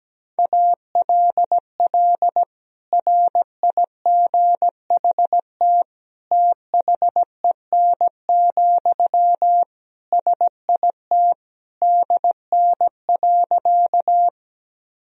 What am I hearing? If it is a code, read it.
Morse code, heard as a keyed tone, about 17 wpm, 700 Hz. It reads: ALL RIGHT THEN, SIT DN.
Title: Sentences from Top 500 Words and Abbreviations 17wpm